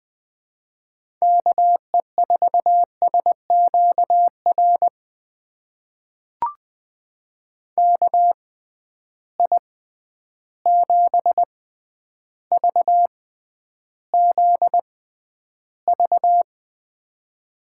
Morse code, heard as a keyed tone, about 20 wpm, spaced as for 10 wpm, 700 Hz. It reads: KE4SQR KI7VZV
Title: USA Call Signs - 6 Characters - 6x Character Spacing - 20wpm